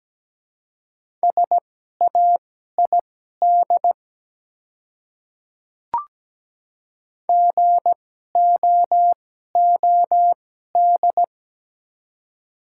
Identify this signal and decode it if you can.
Morse code, heard as a keyed tone, about 17 wpm, spaced as for 12 wpm, 700 Hz. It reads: SAID GOOD
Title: 2x Character Spacing - Top 200 Words - 17wpm